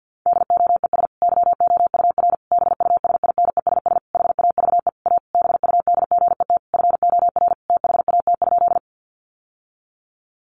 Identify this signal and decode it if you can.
Morse code, heard as a keyed tone, about 50 wpm, 700 Hz. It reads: DOES YOUR BUSINESS HAVE A BUDGET FOR THAT?
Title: Sentences from Top 1200 Words 50wpm